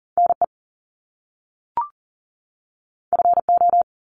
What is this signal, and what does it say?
Morse code, heard as a keyed tone, about 38 wpm, 700 Hz. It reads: NE FO
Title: Top Two Letter Combinations in Words 40wpm